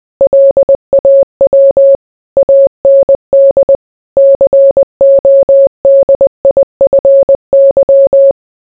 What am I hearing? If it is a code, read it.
Morse code, heard as a keyed tone, about 20 wpm, 550 Hz. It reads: LAW AND CODIFY